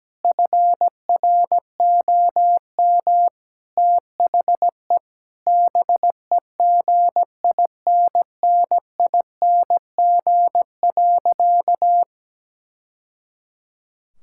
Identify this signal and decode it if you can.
Morse code, heard as a keyed tone, about 17 wpm, 700 Hz. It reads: FROM THE BEGINNING.
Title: Sentences from Top 1000 Words - Review Part 5 - 17wpm